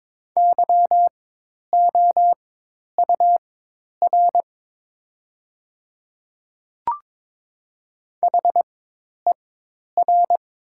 Morse code, heard as a keyed tone, about 22 wpm, spaced as for 10 wpm, 700 Hz. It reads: YOUR HER
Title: 4x Character Spacing - Top 100 Words - 22wpm